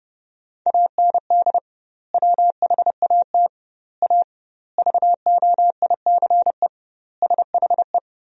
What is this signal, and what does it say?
Morse code, heard as a keyed tone, about 30 wpm, 700 Hz. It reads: AND WHAT A VOICE SHE
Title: 2x Word-Spacing - Sentences from Top 500 Words - 30wpm